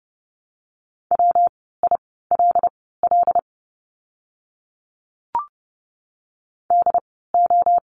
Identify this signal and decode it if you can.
Morse code, heard as a keyed tone, about 30 wpm, spaced as for 17 wpm, 700 Hz. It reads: WILL DO
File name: course / 30:10wpm / 115 Lesson 19f - Letter w - Words 30wpm